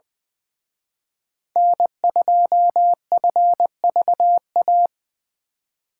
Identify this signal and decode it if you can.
Morse code, heard as a keyed tone, about 24 wpm, 700 Hz. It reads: N2FVA
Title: Speed Racing - USA 1x3 Call Signs 20wpm